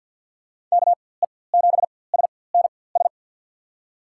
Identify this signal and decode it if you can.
Morse code, heard as a keyed tone, about 52 wpm, spaced as for 23 wpm, 700 Hz. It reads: KE7SNS
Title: USA Call Signs - 6 Characters - 4x Character Spacing - 50wpm